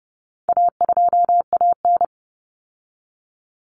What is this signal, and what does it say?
Morse code, heard as a keyed tone, about 30 wpm, 700 Hz. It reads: A2AN
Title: Common Cut Numbers - 4-Digit 30wpm